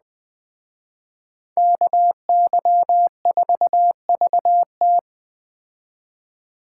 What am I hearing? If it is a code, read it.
Morse code, heard as a keyed tone, about 20 wpm, 700 Hz. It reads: KY4VT
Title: Speed Racing - USA 2x2 Call Signs 17wpm